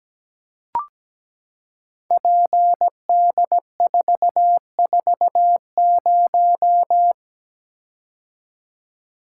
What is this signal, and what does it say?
Morse code, heard as a keyed tone, about 17 wpm, 700 Hz. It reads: PD440
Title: Code Groups - 5 Letter:Numbers - 17wpm